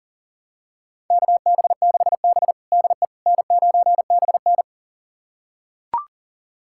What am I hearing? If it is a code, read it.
Morse code, heard as a keyed tone, about 40 wpm, 700 Hz. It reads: KB6B DE N9BN
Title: USA Callsign DE Callsign - 4 Characters - 40wpm